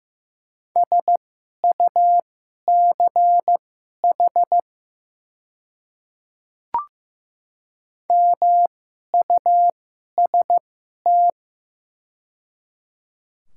Morse code, heard as a keyed tone, about 15 wpm, spaced as for 11 wpm, 700 Hz. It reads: SUCH MUST
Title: 2x Character Spacing - Top 500 Words - 15wpm